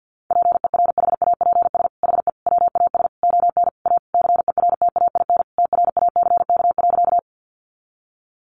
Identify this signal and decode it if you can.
Morse code, heard as a keyed tone, about 48 wpm, 700 Hz. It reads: PERHAPS HE WAS ON A CERTAIN TRACK.
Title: Sentences from Top 1100 Words 50wpm